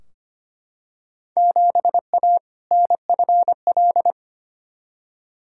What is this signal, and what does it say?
Morse code, heard as a keyed tone, about 25 wpm, 700 Hz. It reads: 7A NFL
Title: Field Day Exchanges Encore - 25wpm